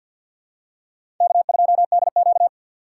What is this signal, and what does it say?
Morse code, heard as a keyed tone, about 53 wpm, 700 Hz. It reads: K2DQ
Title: USA Call Signs - 4 Characters - 2x Character Spacing - 50wpm